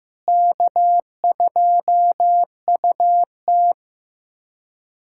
Morse code, heard as a keyed tone, about 15 wpm, 700 Hz. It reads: K2UT
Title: USA Callsign DE Callsign - 4 Characters - 15wpm